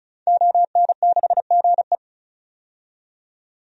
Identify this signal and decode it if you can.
Morse code, heard as a keyed tone, about 36 wpm, 700 Hz. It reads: ON6GE
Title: DXCC-100 Call Signs - 5 Characters - 35wpm